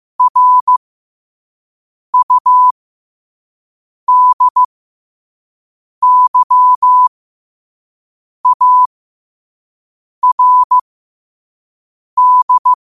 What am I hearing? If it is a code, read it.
Morse code, heard as a keyed tone, about 15 wpm, spaced as for 5 wpm, 1000 Hz. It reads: RUDYARD
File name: qotdfiles / 5wpm